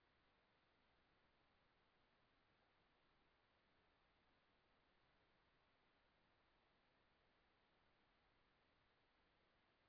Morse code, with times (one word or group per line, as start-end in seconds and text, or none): none